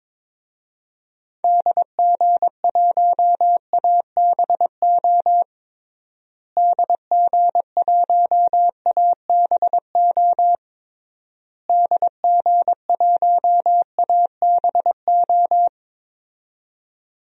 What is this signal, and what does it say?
Morse code, heard as a keyed tone, about 23 wpm, 700 Hz. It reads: DG1ABO DG1ABO DG1ABO
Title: DXCC-100 Call Signs - 6 Characters - Repeated 3x - 22wpm